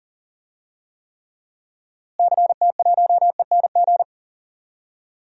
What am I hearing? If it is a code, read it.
Morse code, heard as a keyed tone, about 40 wpm, 700 Hz. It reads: CT1ENG
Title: DXCC-100 Call Signs - 6 Characters - Repeated 2x - 40wpm